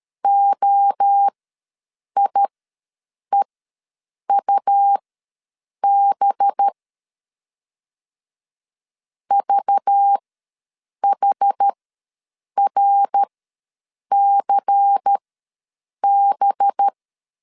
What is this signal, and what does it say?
Morse code, heard as a keyed tone, about 13 wpm, spaced as for 7 wpm, 800 Hz. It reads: OIEUB VHRCB
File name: LEZIONI / 5 LBVC / 5 LBVC ws 12wpm 3159